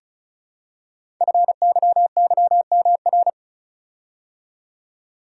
Morse code, heard as a keyed tone, about 36 wpm, 700 Hz. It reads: FYYMR
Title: Code Groups - 5 Letter:Numbers - 35wpm